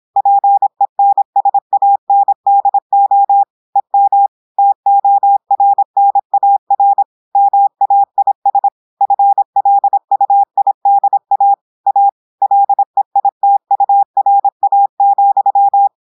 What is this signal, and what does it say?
Morse code, heard as a keyed tone, about 26 wpm, 800 Hz. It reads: PENSANDO EM TORNAR MAIS FLUIDA A LEITURA,
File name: audios / portugais / portugais texte en clair 1 vitesse 26 mots minutes